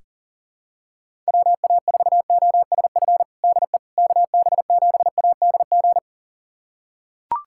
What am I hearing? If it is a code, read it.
Morse code, heard as a keyed tone, about 40 wpm, 700 Hz. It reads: WA4OSF DE KB7ADG
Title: USA Callsign DE Callsign - 6 Characters - 40wpm